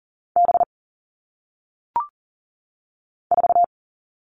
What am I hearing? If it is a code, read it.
Morse code, heard as a keyed tone, about 40 wpm, 700 Hz. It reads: B 4
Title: All Cut Numbers - 1-Digit 40wpm